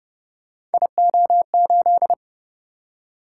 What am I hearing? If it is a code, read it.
Morse code, heard as a keyed tone, about 29 wpm, 700 Hz. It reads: IO8